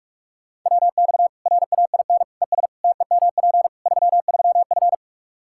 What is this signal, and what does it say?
Morse code, heard as a keyed tone, about 45 wpm, 700 Hz. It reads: WX RAIN ES TEMP 33F
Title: Practice QSO - Exchange 2 - Info - 45wpm